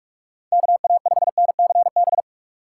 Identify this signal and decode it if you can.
Morse code, heard as a keyed tone, about 46 wpm, 700 Hz. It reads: KA5NCB
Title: USA Call Signs - 6 Characters - 3x Character Spacing - 45wpm